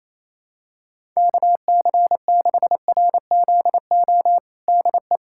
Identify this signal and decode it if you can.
Morse code, heard as a keyed tone, about 28 wpm, 700 Hz. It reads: KC6RZO DE
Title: USA Callsign DE Callsign - 6 Characters - 28wpm